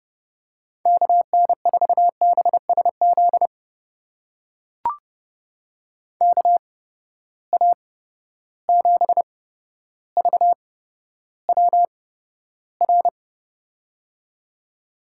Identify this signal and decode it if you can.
Morse code, heard as a keyed tone, about 30 wpm, spaced as for 13 wpm, 700 Hz. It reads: KN4BSZ KA7VWR
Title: USA Call Signs - 6 Characters - 8x Character Spacing - 30wpm